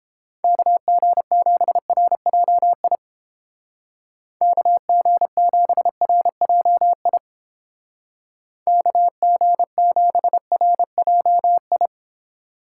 Morse code, heard as a keyed tone, about 30 wpm, 700 Hz. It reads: KG7RJI KG7RJI KG7RJI